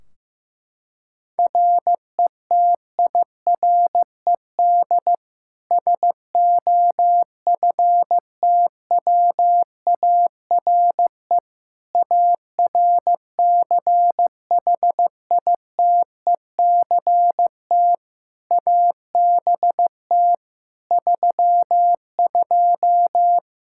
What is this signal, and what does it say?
Morse code, heard as a keyed tone, about 15 wpm, 700 Hz. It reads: RETIRED SOFTWARE ARCHITECT ABT 32
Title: Practice QSO - Exchange 3 - More Info - 15wpm